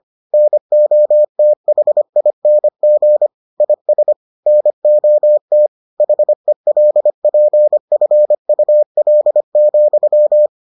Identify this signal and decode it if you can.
Morse code, heard as a keyed tone, about 25 wpm, 600 Hz. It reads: NOTHING IS NOT HELPFUL,